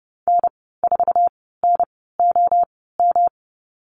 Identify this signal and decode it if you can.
Morse code, heard as a keyed tone, about 30 wpm, spaced as for 17 wpm, 700 Hz. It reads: N4NOM
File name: Lesson 19g - Letter w - Call Signs 30wpm